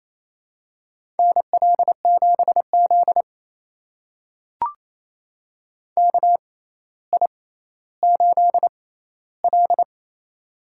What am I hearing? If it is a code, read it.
Morse code, heard as a keyed tone, about 28 wpm, spaced as for 14 wpm, 700 Hz. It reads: NL7Z KI8L
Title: USA Call Signs - 4 Characters - 6x Character Spacing - 28wpm